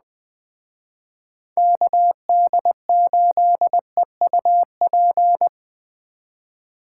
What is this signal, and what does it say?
Morse code, heard as a keyed tone, about 21 wpm, 700 Hz. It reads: KD8EUP
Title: Speed Racing - USA 2x3 Call Signs 17wpm